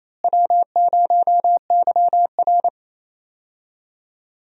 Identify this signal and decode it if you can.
Morse code, heard as a keyed tone, about 28 wpm, 700 Hz. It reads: W0YR
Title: USA Call Signs - 4 Characters - Repeated 2x - 28wpm